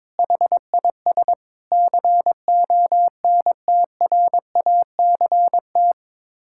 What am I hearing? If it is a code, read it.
Morse code, heard as a keyed tone, about 22 wpm, 700 Hz. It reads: HIS CONTRACT
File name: practice / 22wpm / Sets of 8 Words - 22wpm